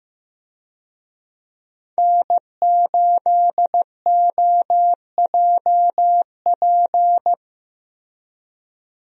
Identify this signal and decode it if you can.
Morse code, heard as a keyed tone, about 15 wpm, 700 Hz. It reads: N8OJP
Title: USA Call Signs - 5 Characters - Repeated 3x - 15wpm